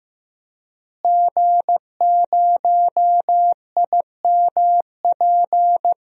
Morse code, heard as a keyed tone, about 15 wpm, 700 Hz. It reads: G0IMP
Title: DXCC-100 Call Signs - 5 Characters - Repeated 2x - 15wpm